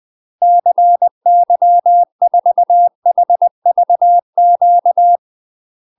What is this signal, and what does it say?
Morse code, heard as a keyed tone, about 20 wpm, 700 Hz. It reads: CY4HVQ
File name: UK MASTER QSOs TRAINING / 20WPM / UK MASTER QSOs 0052